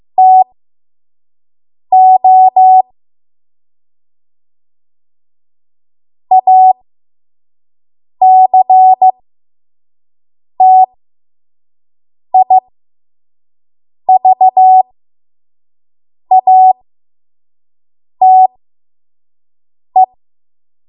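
Morse code, heard as a keyed tone, about 15 wpm, spaced as for 5 wpm, 750 Hz. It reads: TO ACTIVATE
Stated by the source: 5 WPM Morse code practice.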